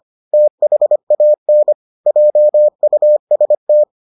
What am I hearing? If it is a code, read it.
Morse code, heard as a keyed tone, about 25 wpm, 600 Hz. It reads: THAN JUST